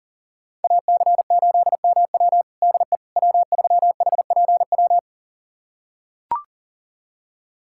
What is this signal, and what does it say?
Morse code, heard as a keyed tone, about 40 wpm, 700 Hz. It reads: AC8MW DE W3HPW
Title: USA Callsign DE Callsign - 5 Characters - 40wpm